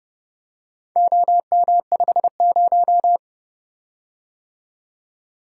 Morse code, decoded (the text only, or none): OM50